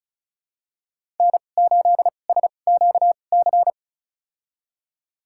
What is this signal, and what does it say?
Morse code, heard as a keyed tone, about 36 wpm, spaced as for 25 wpm, 700 Hz. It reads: N8SQC
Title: USA Call Signs - 5 Characters - 2x Character Spacing - 35wpm